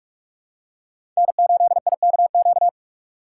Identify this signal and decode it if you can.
Morse code, heard as a keyed tone, about 42 wpm, 700 Hz. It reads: N8IKQ